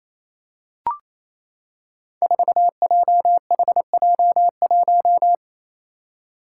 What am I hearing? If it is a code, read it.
Morse code, heard as a keyed tone, about 28 wpm, 700 Hz. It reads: 4JHJ1
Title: Five Challenging Letter-Number Combinations 28wpm